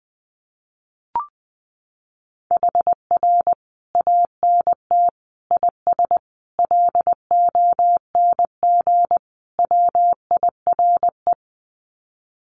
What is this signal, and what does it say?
Morse code, heard as a keyed tone, about 20 wpm, 700 Hz. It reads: HR ANT IS LONG WIRE
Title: Common QSO Phrases 20wpm